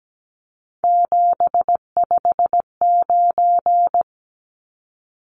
Morse code, heard as a keyed tone, about 17 wpm, 700 Hz. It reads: 759
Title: Numbers - 3-Digit 17wpm